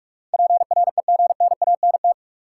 Morse code, heard as a keyed tone, about 46 wpm, 700 Hz. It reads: PREGNANT
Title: Sentences from Top 1000 Words - Review Part 6 - New Word Drill - 45wpm